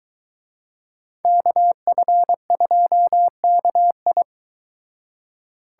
Morse code, heard as a keyed tone, about 23 wpm, 700 Hz. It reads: KF2KI